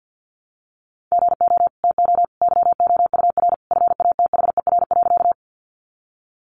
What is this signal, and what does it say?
Morse code, heard as a keyed tone, about 50 wpm, 700 Hz. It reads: GO TO YOUR FATHER.
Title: Sentences from Top 300 Words 50wpm